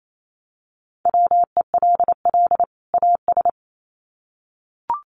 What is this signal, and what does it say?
Morse code, heard as a keyed tone, about 28 wpm, 700 Hz. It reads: WELL AS
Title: Sets of 2 Words 28wpm